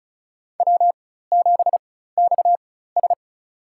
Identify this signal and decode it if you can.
Morse code, heard as a keyed tone, about 36 wpm, spaced as for 16 wpm, 700 Hz. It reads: W7XS
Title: USA Call Signs - 4x Character Spacing - 35wpm